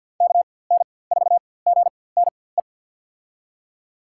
Morse code, heard as a keyed tone, about 52 wpm, spaced as for 24 wpm, 700 Hz. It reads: KN4GNE
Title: USA Call Signs - 6 Characters - 4x Character Spacing - 50wpm